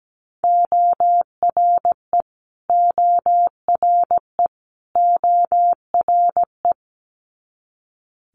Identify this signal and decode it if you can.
Morse code, heard as a keyed tone, about 17 wpm, 700 Hz. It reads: ORE ORE ORE
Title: Multi-letter Phonemes Repeated 3x 17wpm